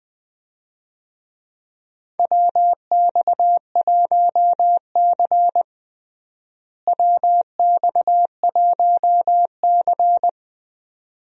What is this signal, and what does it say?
Morse code, heard as a keyed tone, about 20 wpm, 700 Hz. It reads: WX1C WX1C
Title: USA Call Signs - 4 Characters - Repeated 2x - 20wpm